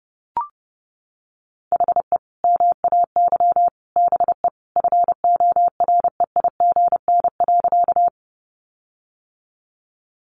Morse code, heard as a keyed tone, about 30 wpm, 700 Hz. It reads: HE MAY BE FOREIGN.
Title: Sentences from Top 1400 Words 30wpm